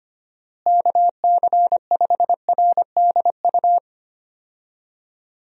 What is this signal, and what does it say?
Morse code, heard as a keyed tone, about 25 wpm, 700 Hz. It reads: KC5RDU
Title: USA Call Signs - 6 Characters - Repeated 3x - 25wpm